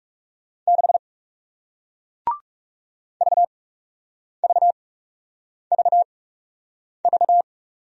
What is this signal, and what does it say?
Morse code, heard as a keyed tone, about 38 wpm, spaced as for 8 wpm, 700 Hz. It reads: 6 VVVV
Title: Speed-Racing Single Letter-Number 30wpm